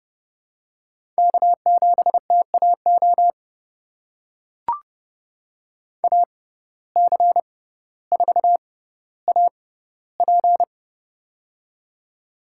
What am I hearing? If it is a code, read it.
Morse code, heard as a keyed tone, about 29 wpm, spaced as for 15 wpm, 700 Hz. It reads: K7TAO AC4AP
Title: USA Call Signs - 5 Characters - 6x Character Spacing - 30wpm